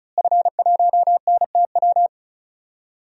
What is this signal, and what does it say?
Morse code, heard as a keyed tone, about 34 wpm, 700 Hz. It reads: F1NTW
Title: DXCC-100 Call Signs - 5 Characters - Repeated 2x - 35wpm